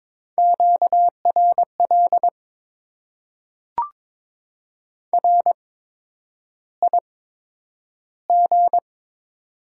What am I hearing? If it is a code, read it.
Morse code, heard as a keyed tone, about 22 wpm, spaced as for 9 wpm, 700 Hz. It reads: QRL RIG
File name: Top 20 QSO Elements - 8x Character Spacing - 22wpm